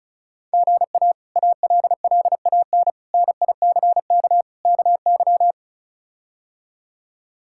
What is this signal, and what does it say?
Morse code, heard as a keyed tone, about 36 wpm, 700 Hz. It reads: GA ALLAN NICK KY
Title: K1USN SST QSO Practice - 35wpm